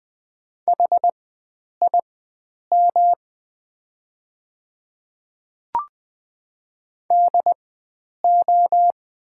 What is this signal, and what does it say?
Morse code, heard as a keyed tone, about 20 wpm, spaced as for 9 wpm, 690 Hz. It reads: HIM DO
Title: 4x Character Spacing - Top 100 Words - 20wpm